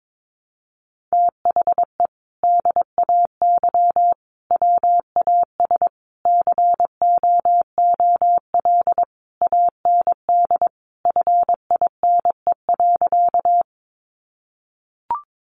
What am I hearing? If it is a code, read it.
Morse code, heard as a keyed tone, about 22 wpm, 700 Hz. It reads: THE DAY WAS COOL AND FINE.
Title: Sentences from Top 1200 Words 22wpm